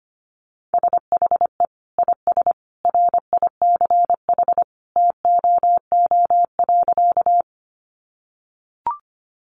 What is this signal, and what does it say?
Morse code, heard as a keyed tone, about 25 wpm, 700 Hz. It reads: SHE IS RICH TOO.
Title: Sentences from Top 1100 Words 25wpm